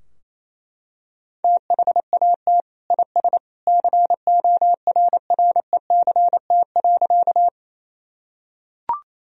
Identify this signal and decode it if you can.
Morse code, heard as a keyed tone, about 28 wpm, 700 Hz. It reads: THAT IS CORRECT.
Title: Sentences from Top 1000 Words - Review Part 2 - 28wpm